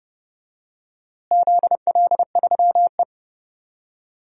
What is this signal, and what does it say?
Morse code, heard as a keyed tone, about 28 wpm, 700 Hz. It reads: ZL3E
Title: DXCC-100 Call Signs - 4 Characters - Repeated 3x - 30wpm